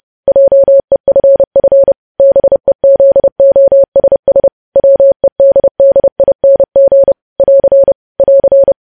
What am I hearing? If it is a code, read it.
Morse code, heard as a keyed tone, about 30 wpm, 550 Hz. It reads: JEFF BEZOSS WEDDING <AR> <AR>
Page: news headlines in morse code at 30 wpm